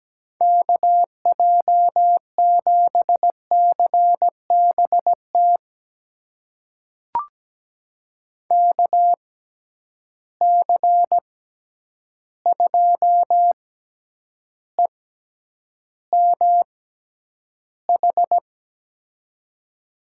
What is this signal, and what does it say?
Morse code, heard as a keyed tone, about 17 wpm, spaced as for 9 wpm, 700 Hz. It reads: KJ7CBT KC2EMH
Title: USA Call Signs - 6 Characters - 6x Character Spacing - 17wpm